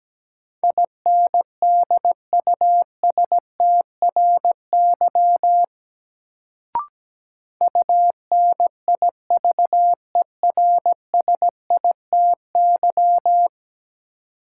Morse code, heard as a keyed tone, about 17 wpm, 700 Hz. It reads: INDUSTRY UNIVERSITY
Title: Sentences from Top 1300 Words - New Word Drill Alt - 17wpm